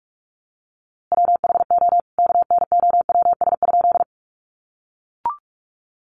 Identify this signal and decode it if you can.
Morse code, heard as a keyed tone, about 44 wpm, 700 Hz. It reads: WHO KNOWS?
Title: A Sound Of Thunder 45wpm